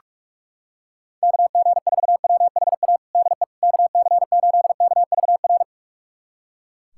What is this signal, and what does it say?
Morse code, heard as a keyed tone, about 44 wpm, 700 Hz. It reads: KG4WHA DE KC8KVR
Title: USA Callsign DE Callsign - 6 Characters - 45wpm